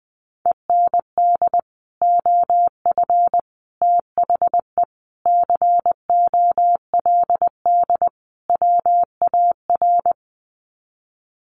Morse code, heard as a keyed tone, about 20 wpm, 700 Hz. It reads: END OF THE COLD WAR
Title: Sets of 5 Words Encore 20wpm